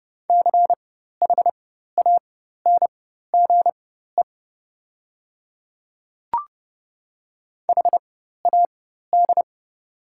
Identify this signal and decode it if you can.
Morse code, heard as a keyed tone, about 30 wpm, spaced as for 14 wpm, 700 Hz. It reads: CHANGE HAD
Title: 4x Character Spacing - Top 300 Words - 30wpm